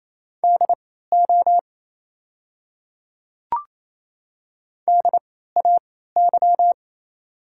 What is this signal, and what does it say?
Morse code, heard as a keyed tone, about 28 wpm, spaced as for 16 wpm, 700 Hz. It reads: DO DAY
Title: 3x Character Spacing - Top 500 Words - 28wpm